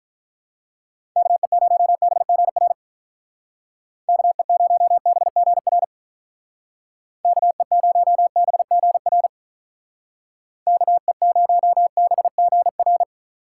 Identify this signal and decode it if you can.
Morse code, heard as a keyed tone, about 44 wpm, 700 Hz. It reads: KE0BGR KE0BGR KE0BGR KE0BGR